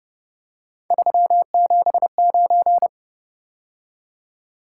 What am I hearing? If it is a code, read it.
Morse code, heard as a keyed tone, about 29 wpm, 700 Hz. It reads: 379